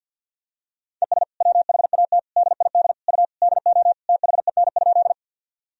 Morse code, heard as a keyed tone, about 52 wpm, 700 Hz. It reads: ES WHAT DID U DO THEN?
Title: Sentences from Top 100 Words and Abbreviations 50wpm